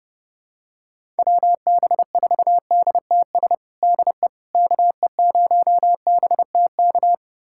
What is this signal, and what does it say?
Morse code, heard as a keyed tone, about 32 wpm, 700 Hz. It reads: WB4DTS DE KE0BTK
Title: USA Callsign DE Callsign - 6 Characters - 30wpm